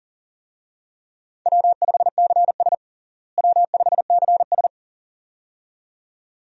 Morse code, heard as a keyed tone, about 40 wpm, 700 Hz. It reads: W5CS W5CS
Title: USA Call Signs - 4 Characters - Repeated 2x - 40wpm